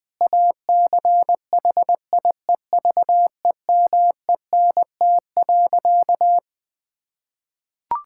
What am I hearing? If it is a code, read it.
Morse code, heard as a keyed tone, about 21 wpm, 700 Hz. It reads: ACHIEVEMENT.
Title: Difficult Sentences with Hints - 20wpm